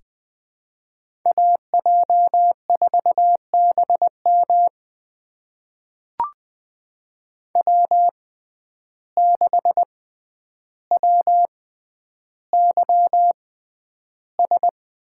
Morse code, heard as a keyed tone, about 20 wpm, spaced as for 10 wpm, 700 Hz. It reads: AJ4BM W6WYS
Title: USA Call Signs - 5 Characters - 6x Character Spacing - 20wpm